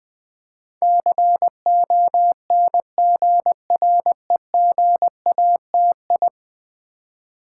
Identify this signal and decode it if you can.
Morse code, heard as a keyed tone, about 20 wpm, 700 Hz. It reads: CONGREGATI
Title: Build A Word - 5 Letters At A Time - 12-Letter Words - 20wpm